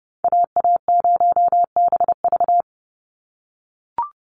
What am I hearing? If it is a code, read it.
Morse code, heard as a keyed tone, about 30 wpm, 700 Hz. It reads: AA0BV
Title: NA SOTA Activator Callsigns 30wpm